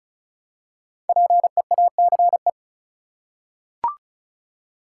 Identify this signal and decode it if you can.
Morse code, heard as a keyed tone, about 36 wpm, 700 Hz. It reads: PEACE
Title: Five Letter Words - 35wpm